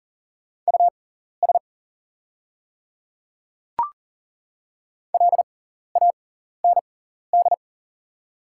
Morse code, heard as a keyed tone, about 38 wpm, spaced as for 14 wpm, 690 Hz. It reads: US LAND
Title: 6x Character Spacing - Top 300 Words - 40wpm